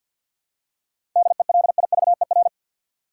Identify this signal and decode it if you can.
Morse code, heard as a keyed tone, about 53 wpm, 700 Hz. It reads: DELIVER
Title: Seven Letter Words - 50wpm